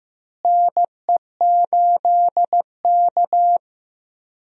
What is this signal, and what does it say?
Morse code, heard as a keyed tone, about 15 wpm, 700 Hz. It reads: NE8K
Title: USA Call Signs - 4 Characters - Repeated 3x - 15wpm